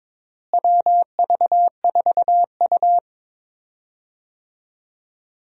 Morse code, heard as a keyed tone, about 22 wpm, 700 Hz. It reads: WV4U
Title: USA Call Signs - 4 Characters - Repeated 2x - 22wpm